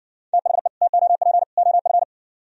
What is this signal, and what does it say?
Morse code, heard as a keyed tone, about 59 wpm, 700 Hz. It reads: THE TOP OF